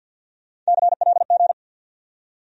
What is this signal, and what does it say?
Morse code, heard as a keyed tone, about 64 wpm, 700 Hz. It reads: CLG